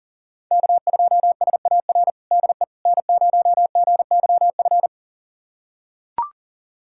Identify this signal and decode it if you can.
Morse code, heard as a keyed tone, about 40 wpm, 700 Hz. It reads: K2SAR DE N0GYF
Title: USA Callsign DE Callsign - 5 Characters - 40wpm